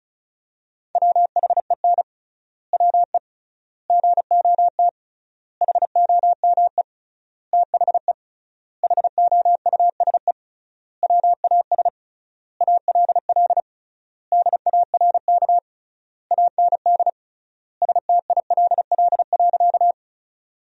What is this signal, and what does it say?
Morse code, heard as a keyed tone, about 34 wpm, 700 Hz. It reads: WHEN WE GOT HOME THE HOUSE WAS ALL DARK AND STILL.
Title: 3x Word-Spacing - Sentences from Top 500 Words - 35wpm